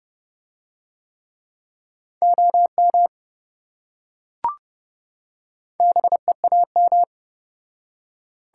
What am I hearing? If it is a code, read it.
Morse code, heard as a keyed tone, about 28 wpm, 700 Hz. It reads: OM BEAM